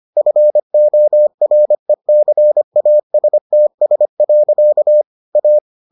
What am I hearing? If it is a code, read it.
Morse code, heard as a keyed tone, about 25 wpm, 600 Hz. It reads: FORECASTS. A